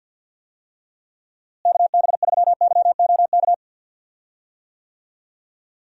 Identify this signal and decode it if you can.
Morse code, heard as a keyed tone, about 52 wpm, 700 Hz. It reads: KB3YOK
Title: USA Call Signs - 6 Characters - Repeated 3x - 50wpm